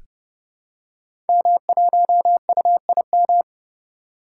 Morse code, heard as a keyed tone, about 30 wpm, 700 Hz. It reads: M1UIM
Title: Code Groups - 5 Letter:Numbers - 30wpm